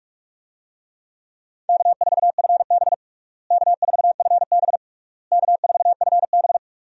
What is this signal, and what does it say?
Morse code, heard as a keyed tone, about 46 wpm, 700 Hz. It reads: K4FB K4FB K4FB
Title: USA Call Signs - 4 Characters - Repeated 3x - 45wpm